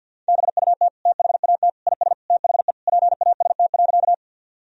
Morse code, heard as a keyed tone, about 50 wpm, 700 Hz. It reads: BUT THAT IS THE PAST.
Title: Sentences from Top 1000 Words - Review Part 3 - 50wpm